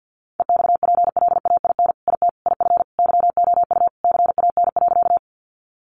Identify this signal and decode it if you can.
Morse code, heard as a keyed tone, about 50 wpm, 700 Hz. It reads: EXPLAIN IT IF YOU CAN.
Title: Sentences from Top 1100 Words 50wpm